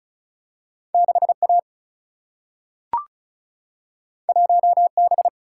Morse code, heard as a keyed tone, about 36 wpm, 700 Hz. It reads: 6A 1B
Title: Single-Letter Single-Number 35wpm